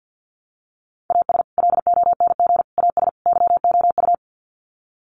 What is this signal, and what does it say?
Morse code, heard as a keyed tone, about 50 wpm, 700 Hz. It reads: AS LONG AS YOU
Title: Sets of 4 Words 50wpm